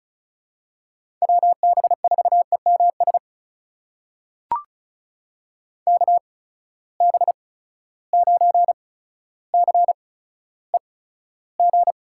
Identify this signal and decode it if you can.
Morse code, heard as a keyed tone, about 34 wpm, spaced as for 15 wpm, 700 Hz. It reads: WB4EMS KB9CEG
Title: USA Call Signs - 6 Characters - 8x Character Spacing - 35wpm